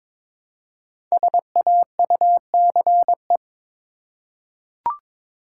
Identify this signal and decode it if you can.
Morse code, heard as a keyed tone, about 22 wpm, 700 Hz. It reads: SAUCE
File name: 5 Letter Words - First 3 Letters Given - 22wpm